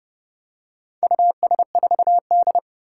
Morse code, heard as a keyed tone, about 29 wpm, 700 Hz. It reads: US4D
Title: Four Challenging Letter-Number Combinations 30wpm